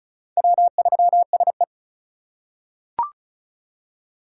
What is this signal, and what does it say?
Morse code, heard as a keyed tone, about 36 wpm, 700 Hz. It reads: W3SE
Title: USA Call Signs - 4 Characters - 4x Character Spacing - 35wpm